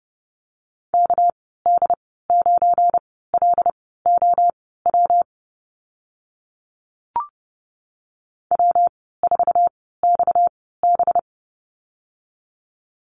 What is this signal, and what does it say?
Morse code, heard as a keyed tone, about 30 wpm, spaced as for 17 wpm, 700 Hz. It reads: KD9LOW W4XB
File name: Lesson 37g - Letter x - Call Signs 30wpm